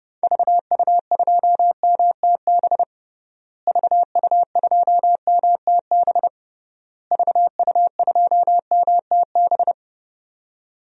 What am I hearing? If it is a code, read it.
Morse code, heard as a keyed tone, about 32 wpm, 700 Hz. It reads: VU2MTB VU2MTB VU2MTB
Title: DXCC-100 Call Signs - 6 Characters - Repeated 3x - 30wpm